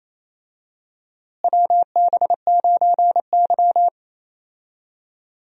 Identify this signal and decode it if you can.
Morse code, heard as a keyed tone, about 27 wpm, 700 Hz. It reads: WB9Y
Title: USA Call Signs - 4 Characters - Repeated 3x - 28wpm